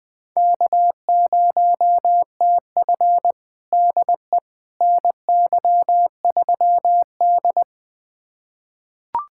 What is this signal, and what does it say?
Morse code, heard as a keyed tone, about 20 wpm, 700 Hz. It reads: K0TF DE NY3D
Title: USA Callsign DE Callsign - 4 Characters - 20wpm